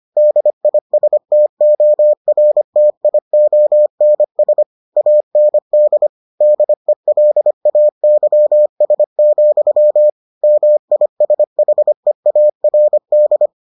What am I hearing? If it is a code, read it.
Morse code, heard as a keyed tone, about 25 wpm, 600 Hz. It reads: DISTORTIONS AND DELAYS, MISHEARD